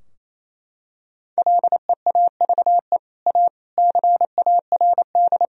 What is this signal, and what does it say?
Morse code, heard as a keyed tone, about 28 wpm, 700 Hz. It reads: LEAVE A CARD
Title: Sentences from Top 1000 Words - Review Part 4 - 28wpm